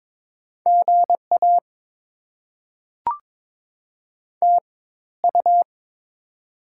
Morse code, heard as a keyed tone, about 22 wpm, spaced as for 14 wpm, 690 Hz. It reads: GA TU
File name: Top 20 QSO Elements - 4x Character Spacing - 22wpm